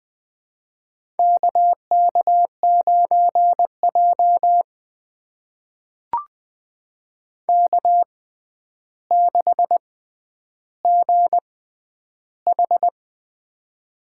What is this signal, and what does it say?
Morse code, heard as a keyed tone, about 21 wpm, spaced as for 10 wpm, 700 Hz. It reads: KK9J K6GH
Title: USA Call Signs - 4 Characters - 6x Character Spacing - 20wpm